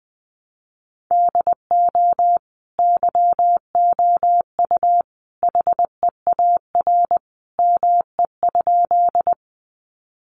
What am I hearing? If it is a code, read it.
Morse code, heard as a keyed tone, about 20 wpm, 700 Hz. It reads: DO YOU HEAR ME?
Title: Sentences from Top 300 Words 20wpm